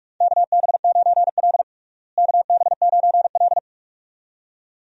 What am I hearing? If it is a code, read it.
Morse code, heard as a keyed tone, about 45 wpm, 700 Hz. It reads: KB9L KB9L
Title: USA Call Signs - 4 Characters - Repeated 2x - 45wpm